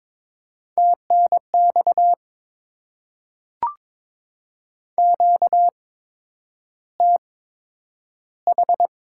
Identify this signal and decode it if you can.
Morse code, heard as a keyed tone, about 23 wpm, spaced as for 10 wpm, 700 Hz. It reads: TNX QTH
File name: Top 20 QSO Elements - 8x Character Spacing - 22wpm